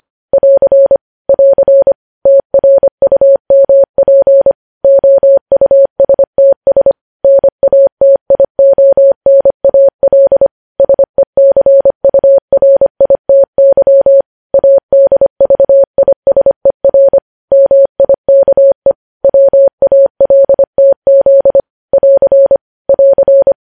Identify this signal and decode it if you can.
Morse code, heard as a keyed tone, about 25 wpm, 550 Hz. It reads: <AR> <AR> TRUMP OUSTS NATIONAL SECURITY ADVISER MIKE WALTZ <AR> <AR>
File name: podcast / News headlines-1x-25x25